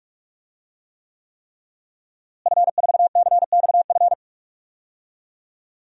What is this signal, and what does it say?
Morse code, heard as a keyed tone, about 42 wpm, 700 Hz. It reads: F4CXF
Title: DXCC-100 Call Signs - 5 Characters - Repeated 3x - 45wpm